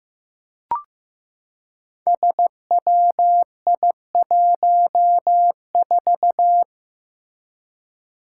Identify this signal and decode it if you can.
Morse code, heard as a keyed tone, about 15 wpm, 700 Hz. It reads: SWI14
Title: Five Challenging Letter-Number Combinations 15wpm